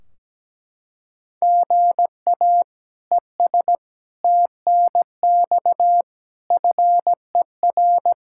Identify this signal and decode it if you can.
Morse code, heard as a keyed tone, about 17 wpm, 700 Hz. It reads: GA ES TNX FER
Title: Practice QSO - Exchange 1 - Beginning - 17wpm